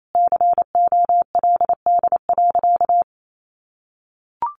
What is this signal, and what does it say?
Morse code, heard as a keyed tone, about 28 wpm, 700 Hz. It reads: COLD.
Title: Sentences from Top 1400 Words 28wpm